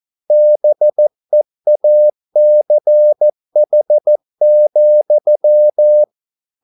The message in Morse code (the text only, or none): BEACH,